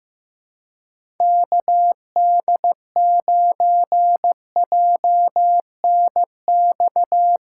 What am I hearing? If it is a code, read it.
Morse code, heard as a keyed tone, about 15 wpm, 700 Hz. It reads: KD9JNX
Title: USA Callsign DE Callsign 15wpm